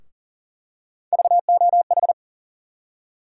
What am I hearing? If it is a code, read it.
Morse code, heard as a keyed tone, about 39 wpm, 700 Hz. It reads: VOH
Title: USA Call Sign Postfixes - 3 Letters 40wpm